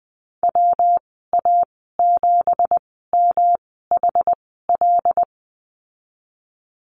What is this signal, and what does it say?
Morse code, heard as a keyed tone, about 20 wpm, spaced as for 15 wpm, 700 Hz. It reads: WA7MHL
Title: 213 Lesson 34g - Number 8 - Call Signs 20wpm